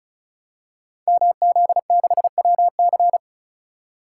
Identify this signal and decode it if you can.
Morse code, heard as a keyed tone, about 36 wpm, 700 Hz. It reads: MZ6WC
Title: Code Groups - 5 Letter:Numbers - 35wpm